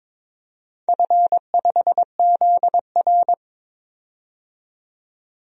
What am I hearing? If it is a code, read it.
Morse code, heard as a keyed tone, about 22 wpm, 700 Hz. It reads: F5ZR
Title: DXCC-100 Call Signs - 4 Characters - 22wpm